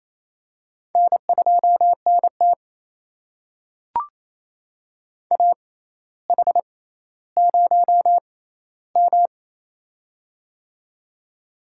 Morse code, heard as a keyed tone, about 28 wpm, spaced as for 14 wpm, 700 Hz. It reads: N2NT AH0M
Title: USA Call Signs - 4 Characters - 6x Character Spacing - 28wpm